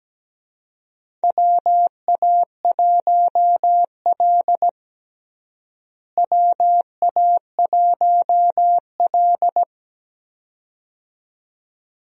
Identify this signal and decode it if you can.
Morse code, heard as a keyed tone, about 17 wpm, 700 Hz. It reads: WA1L WA1L
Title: USA Call Signs - 4 Characters - Repeated 2x - 17wpm